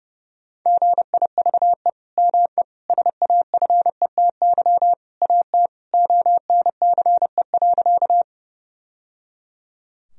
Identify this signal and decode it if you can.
Morse code, heard as a keyed tone, about 32 wpm, 700 Hz. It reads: GIVE ME SAFETY AT ONCE.
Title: Sentences from Top 1000 Words - Review Part 4 - 30wpm